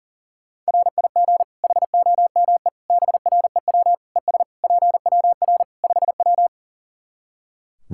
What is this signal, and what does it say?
Morse code, heard as a keyed tone, about 40 wpm, 700 Hz. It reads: RIG HOME BREW ES PWR 5W
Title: Practice QSO - Exchange 2 - Info - 40wpm